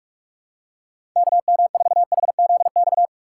Morse code, heard as a keyed tone, about 44 wpm, 700 Hz. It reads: KM4HZX
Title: USA Call Signs - 6 Characters - 6x Character Spacing - 45wpm